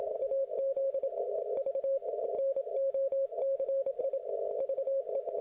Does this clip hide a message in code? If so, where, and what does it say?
Yes: Morse code from 0.2 to 5.2 s, heard as a keyed tone, about 27 wpm, 540 Hz. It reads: AZ V NOCI VE